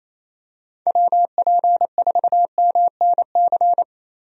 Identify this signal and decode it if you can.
Morse code, heard as a keyed tone, about 27 wpm, 700 Hz. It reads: WP4MNC
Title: USA Call Signs - 6 Characters - Repeated 2x - 28wpm